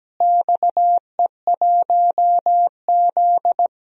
Morse code, heard as a keyed tone, about 17 wpm, 700 Hz. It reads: XE1Z
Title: DXCC-100 Call Signs - 4 Characters - Repeated 2x - 17wpm